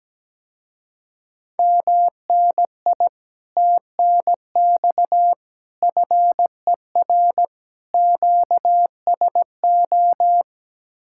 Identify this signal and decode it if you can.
Morse code, heard as a keyed tone, about 17 wpm, 700 Hz. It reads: MNI TNX FER QSO
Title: Speed-Racing Common QSO Phrases - 17wpm